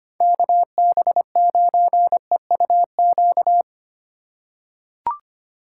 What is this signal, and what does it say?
Morse code, heard as a keyed tone, about 25 wpm, 700 Hz. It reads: KB9EUQ
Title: USA Call Signs - 4x Character Spacing - 25wpm